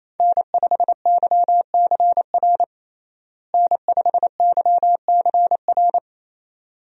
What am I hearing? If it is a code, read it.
Morse code, heard as a keyed tone, about 28 wpm, 700 Hz. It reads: N5YCR N5YCR
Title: USA Call Signs - 5 Characters - Repeated 2x - 28wpm